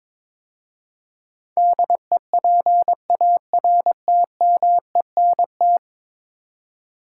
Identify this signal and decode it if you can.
Morse code, heard as a keyed tone, about 21 wpm, 700 Hz. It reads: DEPARTMENT
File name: practice / 22wpm / 10 Letter Words - First 3 Letters Given - 22wpm